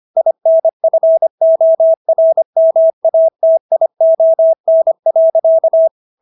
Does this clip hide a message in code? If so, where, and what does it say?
Yes: Morse code from 0.2 to 5.9 s, heard as a keyed tone, about 25 wpm, 650 Hz. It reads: INFORMATION.